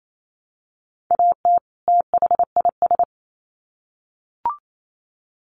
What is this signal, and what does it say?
Morse code, heard as a keyed tone, about 28 wpm, 700 Hz. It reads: AT THIS
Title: Sets of 2 Words 28wpm